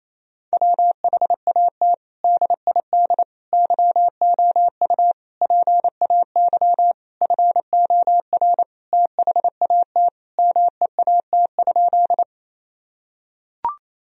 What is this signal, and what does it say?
Morse code, heard as a keyed tone, about 27 wpm, 700 Hz. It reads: WHAT DID YOU PAY FOR THAT MEAT?
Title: Sentences from Top 1000 Words - Review Part 3 - 28wpm